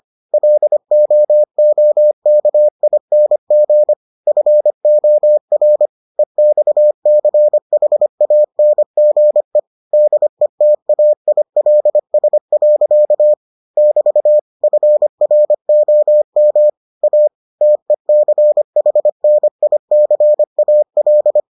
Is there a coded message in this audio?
Yes: Morse code from 0.3 to 21.4 s, heard as a keyed tone, about 25 wpm, 600 Hz. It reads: LOOKING FOR EXCHANGE DETAILS. <BT> FROM A TECHNICAL